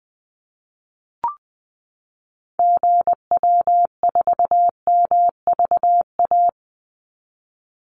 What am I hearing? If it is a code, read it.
Morse code, heard as a keyed tone, about 20 wpm, 700 Hz. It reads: GW4MVA
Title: Call Signs - 20wpm